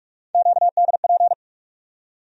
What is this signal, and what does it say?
Morse code, heard as a keyed tone, about 46 wpm, 700 Hz. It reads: QDP
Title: USA Call Sign Postfixes - 3 Letters 45wpm